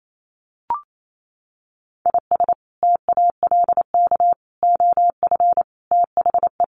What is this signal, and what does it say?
Morse code, heard as a keyed tone, about 28 wpm, 700 Hz. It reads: IS TALK OF THE
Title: Sets of 5 Words 28wpm